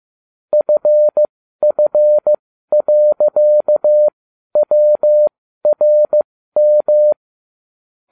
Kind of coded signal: Morse code